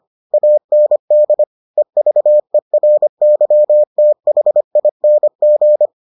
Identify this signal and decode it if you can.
Morse code, heard as a keyed tone, about 25 wpm, 600 Hz. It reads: AND EVERYTHING